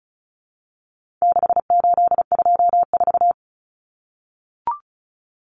Morse code, heard as a keyed tone, about 34 wpm, 700 Hz. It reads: 6824
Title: Common Cut Numbers - 4-Digit 35wpm